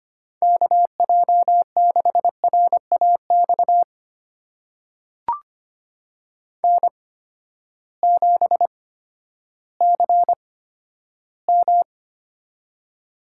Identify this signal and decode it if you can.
Morse code, heard as a keyed tone, about 25 wpm, spaced as for 12 wpm, 700 Hz. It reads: KJ6RAX N7CM
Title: USA Call Signs - 8x Character Spacing - 25wpm